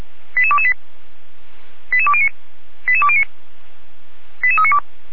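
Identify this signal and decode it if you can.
Five-tone selective call (ZVEI), heard as quick runs of five tones, tones of about 70 ms. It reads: 88228 88209 88209 88392
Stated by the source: ZVEI five-tone selective call 88392 received by SDR, archived by feuerwehr-orlamuende (fire brigade Orlamünde).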